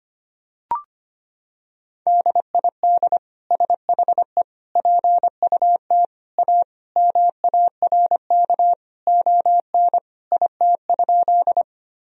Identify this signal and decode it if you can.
Morse code, heard as a keyed tone, about 25 wpm, 700 Hz. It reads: DID SHE PUT A MARK ON IT?
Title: Sentences from Top 1000 Words - Review Part 1 - 25wpm